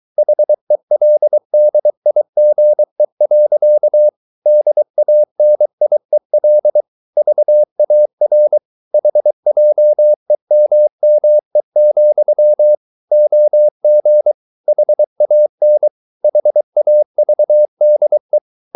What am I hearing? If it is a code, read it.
Morse code, heard as a keyed tone, about 23 wpm, 600 Hz. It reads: HELDIGE. DANIEL VAR HJEMME, OG HAN HAVDE
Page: Ferie 23 wpm